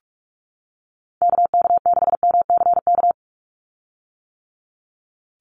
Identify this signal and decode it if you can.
Morse code, heard as a keyed tone, about 44 wpm, 700 Hz. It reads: KK6MCK
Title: USA Call Signs 45wpm